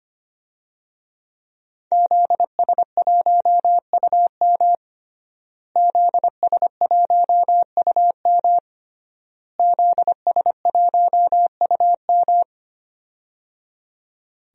Morse code, heard as a keyed tone, about 25 wpm, 700 Hz. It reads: ZS1UM ZS1UM ZS1UM
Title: DXCC-100 Call Signs - 5 Characters - Repeated 3x - 25wpm